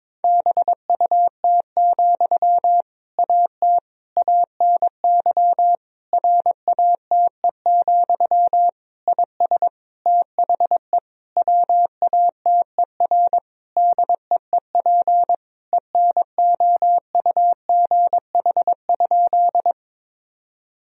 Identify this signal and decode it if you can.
Morse code, heard as a keyed tone, about 22 wpm, 700 Hz. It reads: BUT, AT ANY RATE, IS THE WATER DEEP ENOUGH?
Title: Sentences from Top 1000 Words - Review Part 4 - 22wpm